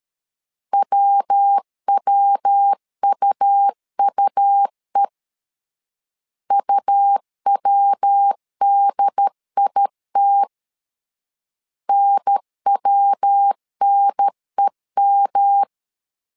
Morse code, heard as a keyed tone, about 13 wpm, 800 Hz. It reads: WWUUE UWDIT NWNEM
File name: LEZIONI / 3 MWRO / 3 MW 12wpm 1633